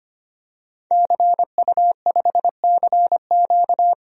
Morse code, heard as a keyed tone, about 25 wpm, 700 Hz. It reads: CU5CQ
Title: DXCC-100 Call Signs - 5 Characters - Repeated 3x - 25wpm